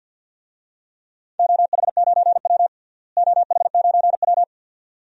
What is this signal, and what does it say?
Morse code, heard as a keyed tone, about 51 wpm, 700 Hz. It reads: OH9W OH9W
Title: DXCC-100 Call Signs - 4 Characters - Repeated 2x - 50wpm